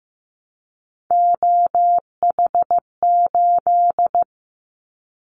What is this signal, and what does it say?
Morse code, heard as a keyed tone, about 15 wpm, 700 Hz. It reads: OH8
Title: Callsign Prefixes for Top 25 DXCC - 15wpm